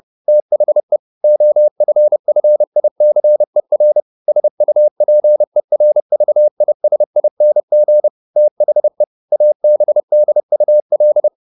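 Morse code, heard as a keyed tone, about 30 wpm, 600 Hz. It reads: THE OFFICER SUPERVISING THE ABDUL
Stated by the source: Morse code at 30 wpm, the novel Never by Ken Follett in English, from CW med Gnister.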